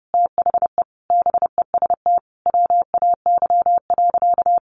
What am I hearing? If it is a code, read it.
Morse code, heard as a keyed tone, about 30 wpm, 700 Hz. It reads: THE BEST WAY.
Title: Sentences from Top 400 Words 30wpm